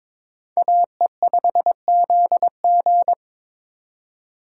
Morse code, heard as a keyed tone, about 26 wpm, 700 Hz. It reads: AE5ZG
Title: Speed Racing - USA 2x2 Call Signs 22wpm